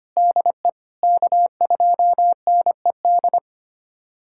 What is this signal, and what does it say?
Morse code, heard as a keyed tone, about 25 wpm, 700 Hz. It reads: DE K2NED
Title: USA Callsign DE Callsign - 5 Characters - 25wpm